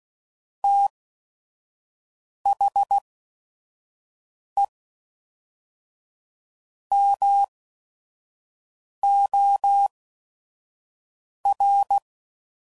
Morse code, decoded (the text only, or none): T H E M O R